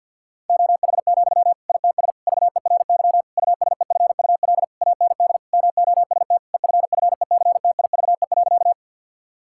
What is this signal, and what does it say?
Morse code, heard as a keyed tone, about 50 wpm, 700 Hz. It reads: OH, ITS VERY USEFUL AND MOST EFFECTIVE.
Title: Sentences from Top 1000 Words - Review Part 6 - 50wpm